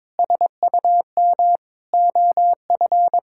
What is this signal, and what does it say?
Morse code, heard as a keyed tone, about 22 wpm, 700 Hz. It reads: SUM OF
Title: Sentences from Top 1000 Words - Review Part 6 - 22wpm